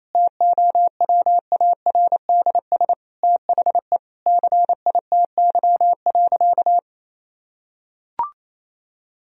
Morse code, heard as a keyed tone, about 28 wpm, 700 Hz. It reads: TOWARDS THE CITY.
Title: Sentences from Top 1000 Words - Review Part 2 - 28wpm